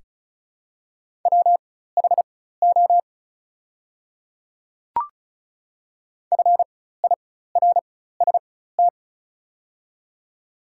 Morse code, heard as a keyed tone, about 36 wpm, spaced as for 16 wpm, 690 Hz. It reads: WHO FIRST
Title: 4x Character Spacing - Top 100 Words - 35wpm